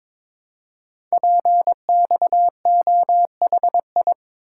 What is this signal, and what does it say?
Morse code, heard as a keyed tone, about 23 wpm, 700 Hz. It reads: PXOHI
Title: Code Groups - 5 Letters - 22wpm